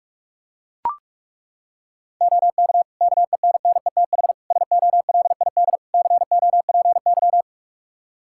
Morse code, heard as a keyed tone, about 43 wpm, 700 Hz. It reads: OK KENNETH SOLID COPY
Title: Practice QSO - Exchange 3 - More Info - 45wpm